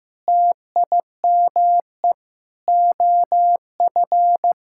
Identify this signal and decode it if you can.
Morse code, heard as a keyed tone, about 15 wpm, 700 Hz. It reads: TIME OF
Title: Sentences from Top 1000 Words - Review Part 5 - 15wpm